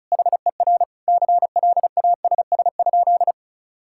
Morse code, heard as a keyed tone, about 36 wpm, 700 Hz. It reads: HER CLASS?
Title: Sentences from Top 1000 Words - Review Part 6 - 35wpm